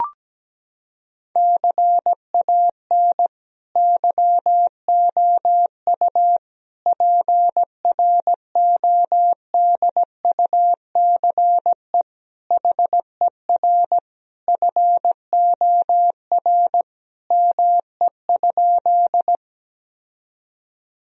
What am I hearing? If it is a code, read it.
Morse code, heard as a keyed tone, about 17 wpm, 700 Hz. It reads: CAN YOU PRODUCE HER FOR ME?
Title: Sentences from Top 1000 Words - Review Part 3 - 17wpm